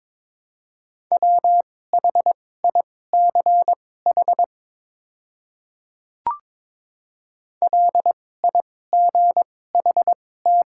Morse code, heard as a keyed tone, about 22 wpm, spaced as for 16 wpm, 700 Hz. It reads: WHICH LIGHT
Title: 2x Character Spacing - Top 200 Words - 22wpm